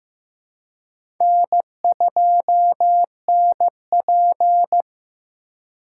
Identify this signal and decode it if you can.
Morse code, heard as a keyed tone, about 15 wpm, 700 Hz. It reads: N2NP
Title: USA Call Signs - 4 Characters - 3x Character Spacing - 15wpm